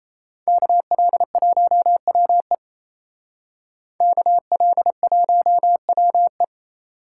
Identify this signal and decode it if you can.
Morse code, heard as a keyed tone, about 34 wpm, 700 Hz. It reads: KL1WE KL1WE
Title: Speed Racing - USA 2x2 Call Signs 28wpm